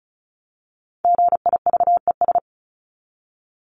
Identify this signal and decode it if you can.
Morse code, heard as a keyed tone, about 34 wpm, 700 Hz. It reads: GIVES
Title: ES:S Suffix - Top 100 Words with Base Words 35wpm